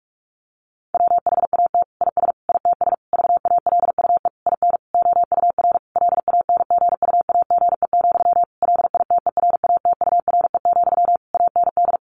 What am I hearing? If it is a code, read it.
Morse code, heard as a keyed tone, about 44 wpm, 700 Hz. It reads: WHAT IS ITS VALUE IN OUR LANGUAGE, LITERATURE, AND
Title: Sentences from Top 600 Words 45wpm